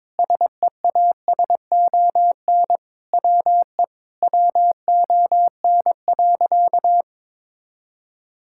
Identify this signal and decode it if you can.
Morse code, heard as a keyed tone, about 22 wpm, 700 Hz. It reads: SEASON WE WON.
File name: Sentences with 7-Word Phrases - 22wpm